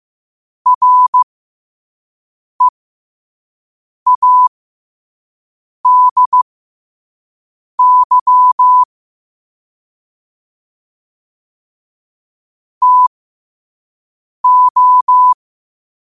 Morse code, heard as a keyed tone, about 15 wpm, spaced as for 5 wpm, 1000 Hz. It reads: READY TO
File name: qotdfiles / 5wpm